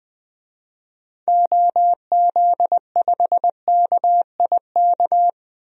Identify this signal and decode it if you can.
Morse code, heard as a keyed tone, about 21 wpm, 700 Hz. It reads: OZ5KIK
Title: DXCC-100 Call Signs - 6 Characters - Repeated 2x - 20wpm